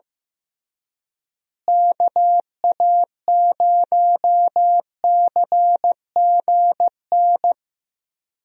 Morse code, heard as a keyed tone, about 16 wpm, 700 Hz. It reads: KA0CGN
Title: USA Call Signs - 6 Characters - Repeated 3x - 15wpm